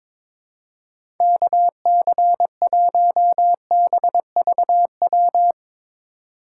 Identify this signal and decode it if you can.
Morse code, heard as a keyed tone, about 22 wpm, 700 Hz. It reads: KC1BVW
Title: USA Call Signs - 6 Characters - 2x Character Spacing - 22wpm